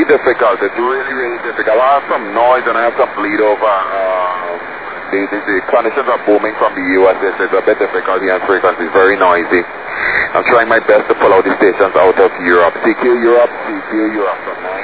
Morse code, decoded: none